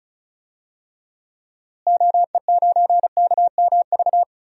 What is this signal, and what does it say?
Morse code, heard as a keyed tone, about 36 wpm, 700 Hz. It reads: OE9KMV
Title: DXCC-100 Call Signs - 6 Characters - Repeated 3x - 35wpm